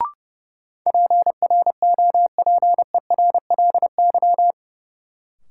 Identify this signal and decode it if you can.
Morse code, heard as a keyed tone, about 32 wpm, 700 Hz. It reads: PROPERLY
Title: Sentences from Top 1100 Words - New Word Drill Alt - 30wpm